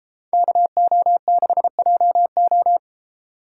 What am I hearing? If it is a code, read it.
Morse code, heard as a keyed tone, about 30 wpm, 700 Hz. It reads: KO6JO